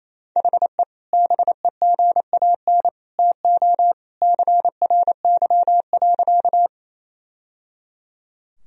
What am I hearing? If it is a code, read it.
Morse code, heard as a keyed tone, about 28 wpm, 700 Hz. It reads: HE BEGAN TO CRY.
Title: Sentences from Top 1000 Words - Review Part 2 - 28wpm